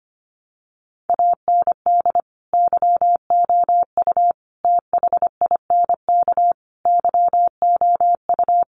Morse code, heard as a keyed tone, about 25 wpm, 700 Hz. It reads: AND YOU THINK YOU
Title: Sentences from Top 1300 Words 25wpm